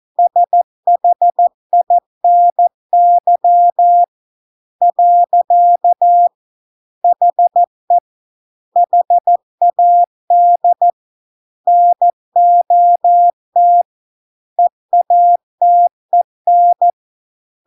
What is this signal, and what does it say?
Morse code, heard as a keyed tone, about 14 wpm, 700 Hz. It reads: SHINY . HE HAD NOT EATEN